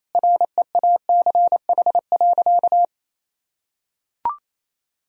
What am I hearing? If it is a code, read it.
Morse code, heard as a keyed tone, about 28 wpm, 700 Hz. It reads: REACH.
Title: Sentences from Top 1000 Words - Review Part 2 - 28wpm